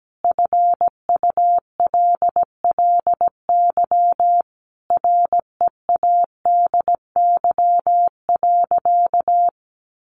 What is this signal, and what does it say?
Morse code, heard as a keyed tone, about 17 wpm, 700 Hz. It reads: FULLY READY.
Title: Sentences from Top 1300 Words 17wpm